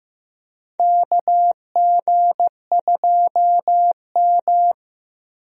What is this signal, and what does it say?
Morse code, heard as a keyed tone, about 16 wpm, 700 Hz. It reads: KG2M
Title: USA Call Signs - 4 Characters - Repeated 2x - 15wpm